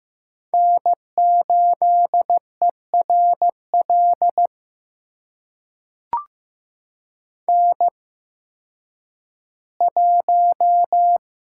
Morse code, heard as a keyed tone, about 15 wpm, spaced as for 10 wpm, 700 Hz. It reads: N8ERL N1
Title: USA Call Signs - 8x Character Spacing - 15wpm